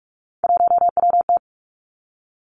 Morse code, heard as a keyed tone, about 46 wpm, 700 Hz. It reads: 1WT
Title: Three Challenging Letter-Number Combinations 45wpm